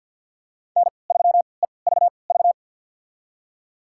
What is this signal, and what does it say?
Morse code, heard as a keyed tone, about 53 wpm, spaced as for 29 wpm, 700 Hz. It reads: N3EVV
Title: USA Call Signs - 5 Characters - 3x Character Spacing - 50wpm